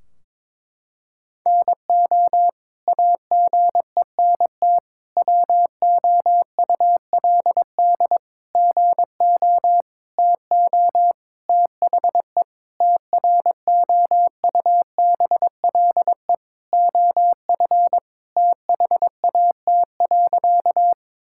Morse code, heard as a keyed tone, about 22 wpm, 700 Hz. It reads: NO AGENT WOULD GO TO THE TROUBLE OF THAT.
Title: Sentences from Top 1600 Words 22wpm